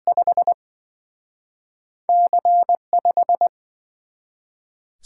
Morse code, heard as a keyed tone, about 27 wpm, spaced as for 24 wpm, 700 Hz. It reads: 5 C5
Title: Speed-Racing - Two Letter-Number Combinations - 20wpm